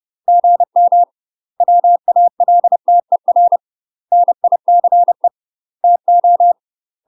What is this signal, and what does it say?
Morse code, heard as a keyed tone, about 30 wpm, 700 Hz. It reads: GM WALTER NICE TO
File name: UK MASTER QSOs TRAINING / 30WPM / UK MASTER QSOs 0036